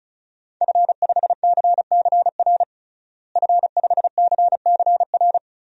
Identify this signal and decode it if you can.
Morse code, heard as a keyed tone, about 34 wpm, 700 Hz. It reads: F5CCR F5CCR
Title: DXCC-100 Call Signs - 5 Characters - Repeated 3x - 35wpm